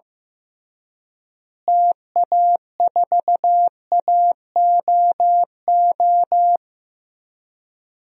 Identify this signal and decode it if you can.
Morse code, heard as a keyed tone, about 15 wpm, 700 Hz. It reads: TA4AOO
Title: DXCC-100 Call Signs - 6 Characters - Repeated 3x - 15wpm